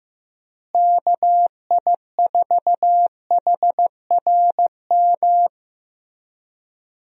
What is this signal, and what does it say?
Morse code, heard as a keyed tone, about 15 wpm, 700 Hz. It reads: KI4HRM
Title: USA Call Signs - 6 Characters - Repeated 3x - 15wpm